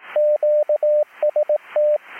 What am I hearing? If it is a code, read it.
Morse code, heard as a keyed tone, about 18 wpm, 600 Hz, with radio noise. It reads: QST